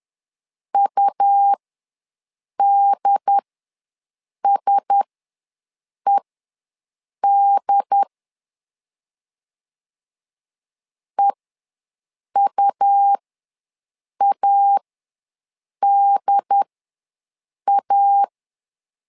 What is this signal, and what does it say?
Morse code, heard as a keyed tone, about 10 wpm, spaced as for 6 wpm, 800 Hz. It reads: UDSED EUADA
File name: LEZIONI / 4 Q1S ws 10wpm 2208